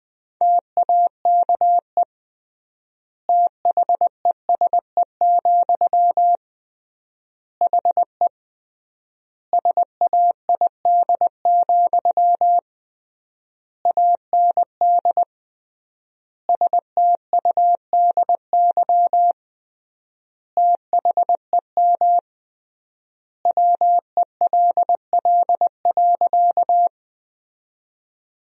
Morse code, heard as a keyed tone, about 20 wpm, 700 Hz. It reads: TAKE THESE, HE SAID, AND STUDY THEM WELL.
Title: 3x Word-Spacing - Sentences from Top 300 Words - 20wpm